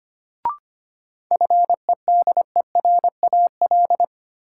Morse code, heard as a keyed tone, about 25 wpm, 700 Hz. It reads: FEDERAL
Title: Sets of 3 Words - New Word Drill - 25wpm